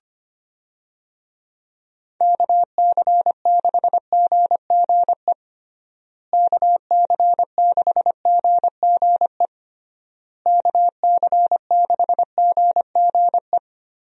Morse code, heard as a keyed tone, about 25 wpm, 700 Hz. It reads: KC6GGE KC6GGE KC6GGE
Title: USA Call Signs - 6 Characters - Repeated 3x - 25wpm